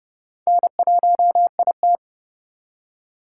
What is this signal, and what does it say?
Morse code, heard as a keyed tone, about 39 wpm, 690 Hz. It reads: N1IT